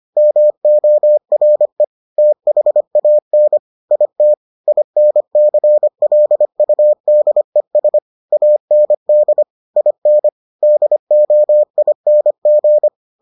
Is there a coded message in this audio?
Yes: Morse code from 0.2 to 12.9 s, heard as a keyed tone, about 25 wpm, 600 Hz. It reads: MORE THAN IT INCLUDES AND IN DOING